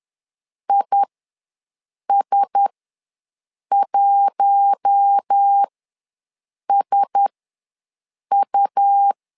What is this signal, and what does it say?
Morse code, heard as a keyed tone, about 10 wpm, spaced as for 6 wpm, 800 Hz. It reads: IS1SU
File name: LEZIONI / 4 Q1S ws 10wpm 2208